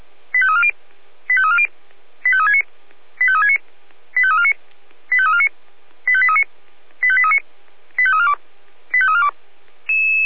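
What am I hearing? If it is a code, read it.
Five-tone selective call (ZVEI), heard as quick runs of five tones, tones of about 70 ms. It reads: 85340 85340 85369 85369 85349 85349 85839 85839 85342 85342
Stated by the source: ZVEI five-tone selective call received via SDR.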